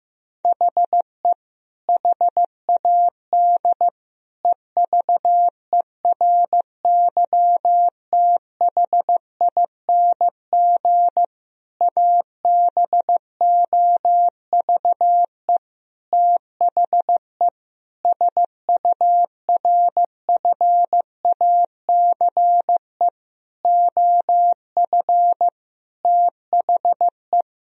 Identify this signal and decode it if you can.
Morse code, heard as a keyed tone, about 15 wpm, 700 Hz. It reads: HE HAD EVERYTHING ABOVE THE SURFACE OF THE
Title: Sentences from Top 1000 Words - Review Part 3 - 15wpm